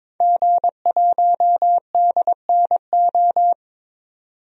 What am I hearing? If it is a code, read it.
Morse code, heard as a keyed tone, about 22 wpm, 700 Hz. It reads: G1DNO
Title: DXCC-100 Call Signs - 5 Characters - 22wpm